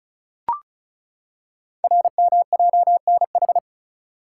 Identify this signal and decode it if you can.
Morse code, heard as a keyed tone, about 36 wpm, 690 Hz. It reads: RMJNH
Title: Code Groups - 5 Letters - 35wpm